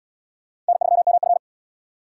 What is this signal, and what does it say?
Morse code, heard as a keyed tone, about 69 wpm, 700 Hz. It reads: N3MF